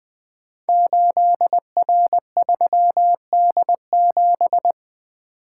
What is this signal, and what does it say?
Morse code, heard as a keyed tone, about 20 wpm, 700 Hz. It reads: 8R3D7
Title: Code Groups - 5 Letter:Numbers - 20wpm